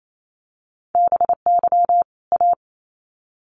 Morse code, heard as a keyed tone, about 28 wpm, 700 Hz. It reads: BY A
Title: Sets of 2 Words 28wpm